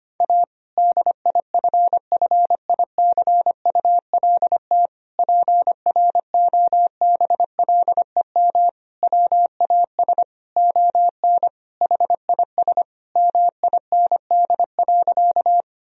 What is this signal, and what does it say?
Morse code, heard as a keyed tone, about 25 wpm, 700 Hz. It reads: A DIFFICULT PROBLEM WAS ON HIS MIND.
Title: Sentences from Top 1000 Words - Review Part 2 - 25wpm